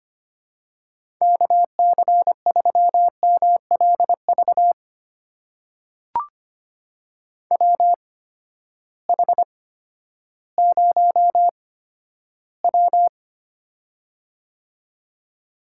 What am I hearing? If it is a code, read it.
Morse code, heard as a keyed tone, about 25 wpm, spaced as for 12 wpm, 700 Hz. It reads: KC3MLV WH0W
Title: USA Call Signs - 8x Character Spacing - 25wpm